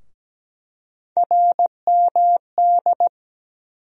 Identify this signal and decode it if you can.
Morse code, heard as a keyed tone, about 17 wpm, 700 Hz. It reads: RMD
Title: USA Call Sign Postfixes - 3 Letters 17wpm